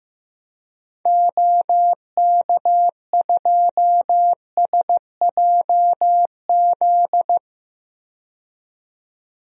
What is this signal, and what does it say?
Morse code, heard as a keyed tone, about 15 wpm, 700 Hz. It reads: OK2SJZ
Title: DXCC-100 Call Signs - 6 Characters - 15wpm